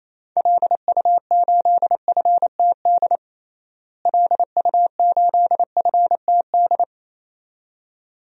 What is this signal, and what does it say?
Morse code, heard as a keyed tone, about 28 wpm, 700 Hz. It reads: LU8FTD LU8FTD
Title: DXCC-100 Call Signs - 6 Characters - Repeated 2x - 28wpm